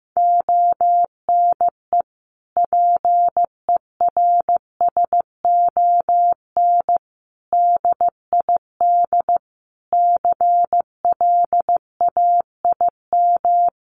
Morse code, heard as a keyed tone, about 15 wpm, 700 Hz. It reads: ONE PERSON DID CLAIM
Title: Sentences from Top 1500 Words 15wpm